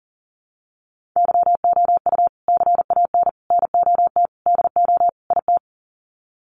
Morse code, heard as a keyed tone, about 40 wpm, 700 Hz. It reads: YOU CAN NOT DO IT
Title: Sets of 5 Words Encore2 40wpm